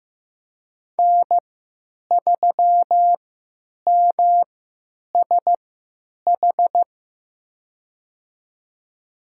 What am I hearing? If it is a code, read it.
Morse code, heard as a keyed tone, about 15 wpm, spaced as for 9 wpm, 700 Hz. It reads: N3MSH
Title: USA Call Signs - 3x Character Spacing - 15wpm